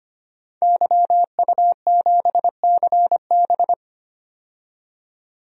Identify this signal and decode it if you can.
Morse code, heard as a keyed tone, about 25 wpm, 700 Hz. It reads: YU7CB
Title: DXCC-100 Call Signs - 5 Characters - 25wpm